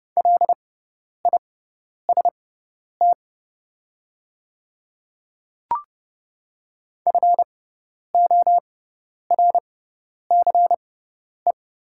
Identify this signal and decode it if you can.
Morse code, heard as a keyed tone, about 30 wpm, spaced as for 10 wpm, 690 Hz. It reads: LIST FORCE
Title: 6x Character Spacing - Top 500 Words - 30wpm